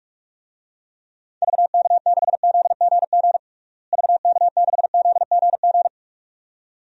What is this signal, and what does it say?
Morse code, heard as a keyed tone, about 46 wpm, 700 Hz. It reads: VK6ZGG VK6ZGG
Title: DXCC-100 Call Signs - 6 Characters - Repeated 2x - 45wpm